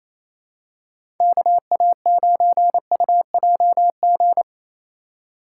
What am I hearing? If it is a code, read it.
Morse code, heard as a keyed tone, about 28 wpm, 700 Hz. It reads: KA9UJG
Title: USA Call Signs - 6 Characters - 3x Character Spacing - 28wpm